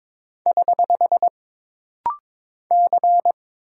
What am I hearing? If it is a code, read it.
Morse code, heard as a keyed tone, about 23 wpm, 700 Hz. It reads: <HH> C
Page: This rapid-fire timing is geared to push your ICR (Instant Character Recognition) skills.